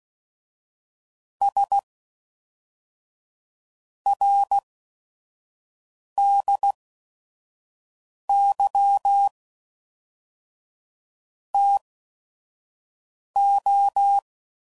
Morse code, heard as a keyed tone, about 16 wpm, spaced as for 5 wpm, 800 Hz. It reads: S RDY TO
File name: CW Code Groups @5WPM / QSOs / QSO WA7QOX 1